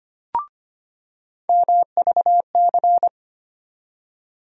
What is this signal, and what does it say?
Morse code, heard as a keyed tone, about 25 wpm, 700 Hz. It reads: MVC
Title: USA Call Sign Postfixes - 3 Letters 25wpm